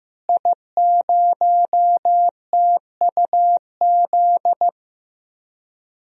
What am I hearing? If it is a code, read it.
Morse code, heard as a keyed tone, about 15 wpm, 700 Hz. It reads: I0TUZ
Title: DXCC-100 Call Signs - 5 Characters - Repeated 3x - 15wpm